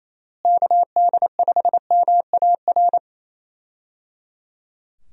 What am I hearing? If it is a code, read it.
Morse code, heard as a keyed tone, about 28 wpm, 700 Hz. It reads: KD5MAR
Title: USA Call Signs - 6 Characters - Repeated 3x - 28wpm